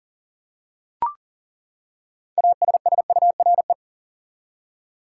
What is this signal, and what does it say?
Morse code, heard as a keyed tone, about 40 wpm, 700 Hz. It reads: ASSURE
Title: Six Letter Words - 40wpm